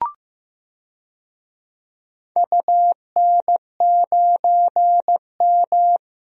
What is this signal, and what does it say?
Morse code, heard as a keyed tone, about 15 wpm, 700 Hz. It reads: UN9M
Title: DXCC-100 Call Signs - 4 Characters - 15wpm